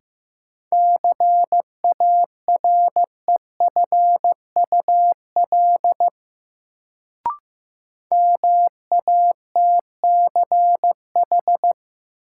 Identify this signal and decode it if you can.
Morse code, heard as a keyed tone, about 15 wpm, 700 Hz. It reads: CAREFUL MATCH
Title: Sentences from Top 1600 Words - New Word Drill Alt - 15wpm